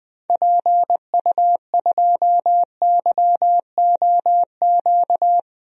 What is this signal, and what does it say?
Morse code, heard as a keyed tone, about 20 wpm, 700 Hz. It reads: PU2YOQ
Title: DXCC-100 Call Signs - 6 Characters - 20wpm